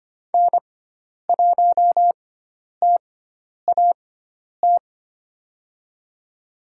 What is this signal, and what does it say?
Morse code, heard as a keyed tone, about 25 wpm, spaced as for 10 wpm, 700 Hz. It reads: N1TAT
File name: Lesson 16g - Letter c - Call Signs 25wpm